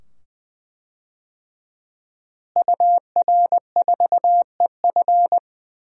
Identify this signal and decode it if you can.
Morse code, heard as a keyed tone, about 20 wpm, 700 Hz. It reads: UR4EF
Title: DXCC-100 Call Signs - 5 Characters - Repeated 2x - 20wpm